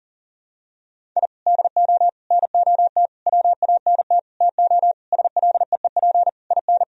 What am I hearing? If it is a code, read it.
Morse code, heard as a keyed tone, about 40 wpm, 700 Hz. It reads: I DO NOT WANT TO SLEEP IN
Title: Sentences from Top 1000 Words - Review Part 2 - 40wpm